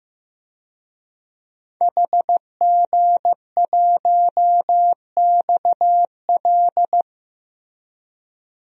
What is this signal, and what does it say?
Morse code, heard as a keyed tone, about 15 wpm, 700 Hz. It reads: HG1XL
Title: DXCC-100 Call Signs - 5 Characters - Repeated 2x - 15wpm